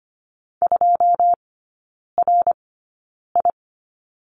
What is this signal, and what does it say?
Morse code, heard as a keyed tone, about 25 wpm, spaced as for 9 wpm, 690 Hz. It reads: 2RI
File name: Lesson 14e - Number 2 - Three Characters 25wpm